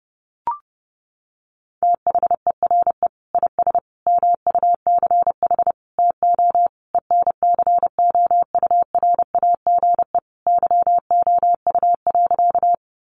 Morse code, heard as a keyed tone, about 30 wpm, 700 Hz. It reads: THERE IS MUCH TO ENCOURAGE YOU.
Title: Sentences from Top 1400 Words 30wpm